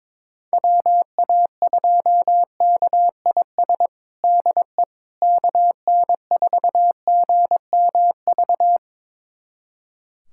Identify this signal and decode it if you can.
Morse code, heard as a keyed tone, about 21 wpm, 700 Hz. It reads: WA2KIS DE KN4GMV
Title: USA Callsign DE Callsign 22wpm